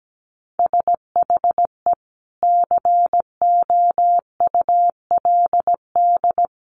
Morse code, heard as a keyed tone, about 17 wpm, 700 Hz. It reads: SHE COULD
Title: Sentences from Top 1100 Words 17wpm